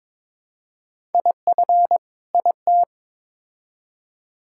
Morse code, heard as a keyed tone, about 22 wpm, 700 Hz. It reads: IF IT
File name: Build A Phrase - 2-Word Phrases - 22wpm